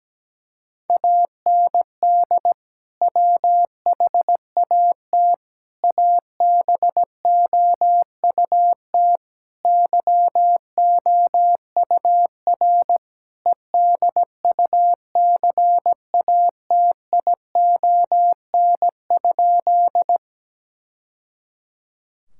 Morse code, heard as a keyed tone, about 17 wpm, 700 Hz. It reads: AND WHAT ABOUT YOUR EDUCATION?
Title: Sentences from Top 1000 Words - Review Part 6 - 17wpm